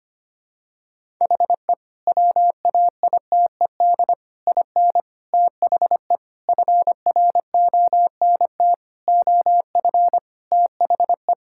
Morse code, heard as a keyed tone, about 25 wpm, 700 Hz. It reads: HE WAITED IN THE FRONT OF THE
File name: practice / 25wpm / Sentences with 4-Word Phrases - 25wpm